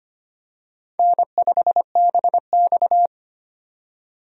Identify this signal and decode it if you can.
Morse code, heard as a keyed tone, about 25 wpm, 700 Hz. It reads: N5BX
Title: USA Call Signs - 4 Characters - 25wpm